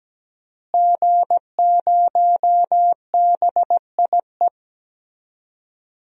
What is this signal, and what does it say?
Morse code, heard as a keyed tone, about 17 wpm, 700 Hz. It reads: G0BIE
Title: DXCC-100 Call Signs - 5 Characters - Repeated 3x - 17wpm